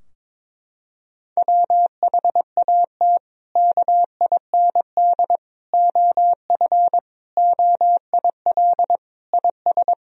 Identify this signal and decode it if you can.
Morse code, heard as a keyed tone, about 21 wpm, 700 Hz. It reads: WHAT KIND OF OIL IS
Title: Sentences from Top 1000 Words - Review Part 4 - 22wpm